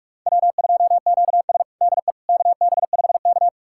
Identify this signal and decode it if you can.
Morse code, heard as a keyed tone, about 46 wpm, 700 Hz. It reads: W2QS DE KB5K
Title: USA Callsign DE Callsign - 4 Characters - 45wpm